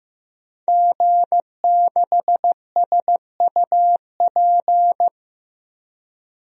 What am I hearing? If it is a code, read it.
Morse code, heard as a keyed tone, about 15 wpm, 700 Hz. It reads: G6SUP
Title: DXCC-100 Call Signs - 5 Characters - 15wpm